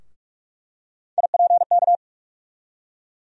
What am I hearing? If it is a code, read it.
Morse code, heard as a keyed tone, about 43 wpm, 700 Hz. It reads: IPK
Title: USA Call Sign Postfixes - 3 Letters 45wpm